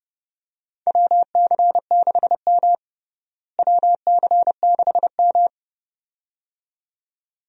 Morse code, heard as a keyed tone, about 32 wpm, 700 Hz. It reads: WC6M WC6M
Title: USA Call Signs - 4 Characters - Repeated 2x - 30wpm